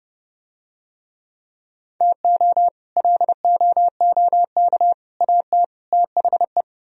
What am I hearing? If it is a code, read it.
Morse code, heard as a keyed tone, about 30 wpm, 700 Hz. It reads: TO LOOK AT THE